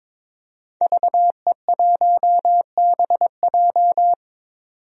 Morse code, heard as a keyed tone, about 22 wpm, 700 Hz. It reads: VE1BJ
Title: DXCC-100 Call Signs - 5 Characters - 22wpm